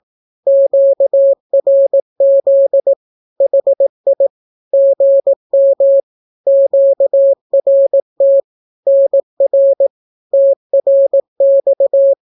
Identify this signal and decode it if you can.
Morse code, heard as a keyed tone, about 18 wpm, 550 Hz. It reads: QRZ HI GM QRT NR TRX
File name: naukapodstaw / skroty / skroty